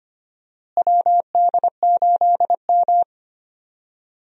Morse code, heard as a keyed tone, about 25 wpm, 700 Hz. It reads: WD8M
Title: USA Call Signs - 4 Characters - Repeated 3x - 25wpm